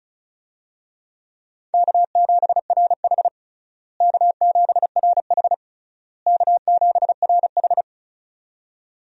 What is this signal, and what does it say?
Morse code, heard as a keyed tone, about 36 wpm, 700 Hz. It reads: K7RH K7RH K7RH
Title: USA Call Signs - 4 Characters - Repeated 3x - 35wpm